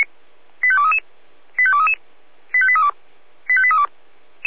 Five-tone selective call (ZVEI), heard as quick runs of five tones, tones of about 70 ms. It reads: 85233 85233 85832 85832
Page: /496/SDR/ZVEI/85232/2023.02.27/